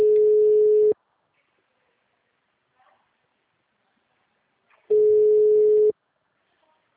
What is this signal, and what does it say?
Ringback tone (Europe, 425 Hz), with speech.
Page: Dracena (Call-Center)